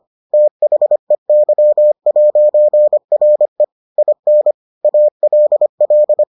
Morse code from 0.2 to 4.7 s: THEY'RE IN

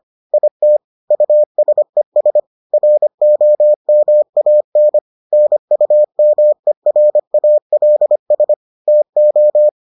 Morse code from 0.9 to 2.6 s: USES

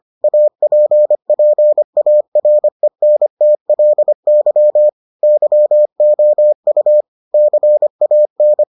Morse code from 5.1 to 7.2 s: YOU

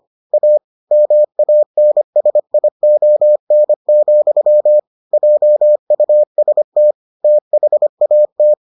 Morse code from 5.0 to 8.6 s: JUST THAT

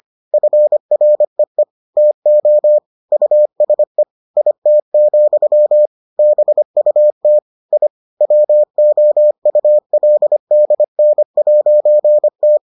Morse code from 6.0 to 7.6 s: BUT

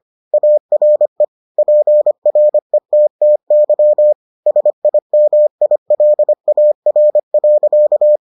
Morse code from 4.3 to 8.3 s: SIMILAR.

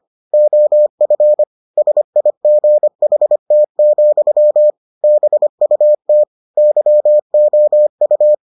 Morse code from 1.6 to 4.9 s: SIGHT,

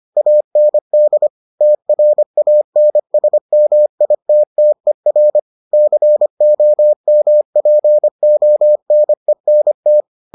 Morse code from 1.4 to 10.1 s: TRANSMITTER COMPONENT

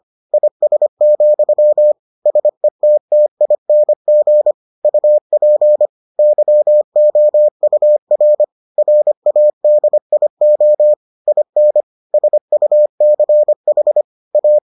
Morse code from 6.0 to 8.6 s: YOUR

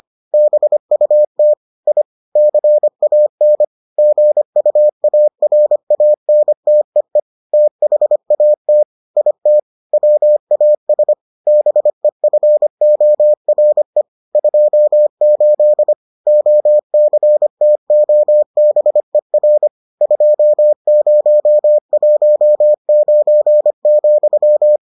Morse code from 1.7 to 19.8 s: I CAN GUARANTEE THAT IT WAS BEFORE 28 OCTOBER

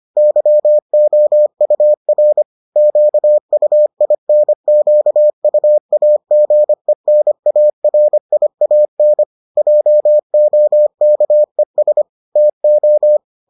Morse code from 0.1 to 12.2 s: YOUR QUINQUAGENARIAN JOKES

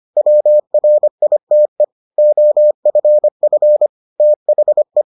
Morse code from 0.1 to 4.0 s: WRITE OFF

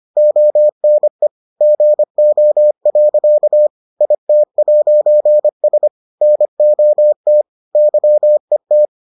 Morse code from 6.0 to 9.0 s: NOT YET

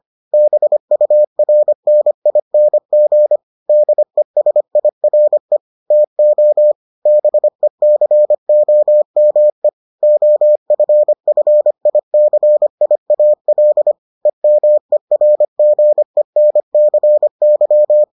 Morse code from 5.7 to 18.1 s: TO BECOME OFFICIAL EMERGENCY

